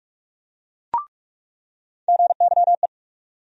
Morse code, heard as a keyed tone, about 44 wpm, 700 Hz. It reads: GYE
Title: USA Call Sign Postfixes - 3 Letters 45wpm